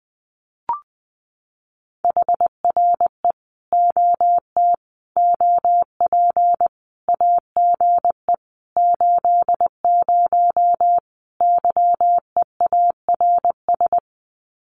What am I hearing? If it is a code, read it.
Morse code, heard as a keyed tone, about 20 wpm, 700 Hz. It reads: HRE OT OP AGE 80 YEARS
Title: Common QSO Phrases 20wpm